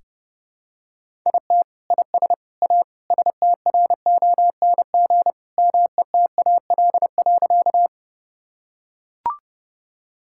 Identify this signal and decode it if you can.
Morse code, heard as a keyed tone, about 30 wpm, 700 Hz. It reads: IT IS A STRONG METAL.
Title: Sentences from Top 1000 Words - Review Part 6 - 30wpm